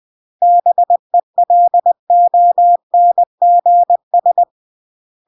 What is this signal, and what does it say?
Morse code, heard as a keyed tone, about 20 wpm, 700 Hz. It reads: BELONGS
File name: CW / jodorowsky / 82 commandments 20-20